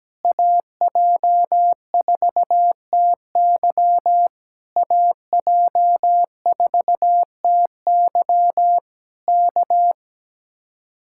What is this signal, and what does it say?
Morse code, heard as a keyed tone, about 17 wpm, 700 Hz. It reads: AJ4TY AJ4TY K
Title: Speed-Racing Common QSO Phrases Encore - 17wpm